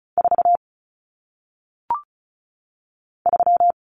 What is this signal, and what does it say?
Morse code, heard as a keyed tone, about 36 wpm, 700 Hz. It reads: 4 3
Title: Numbers - 1-Digit 35wpm